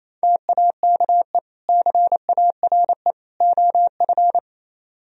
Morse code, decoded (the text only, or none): TAKE CARE OF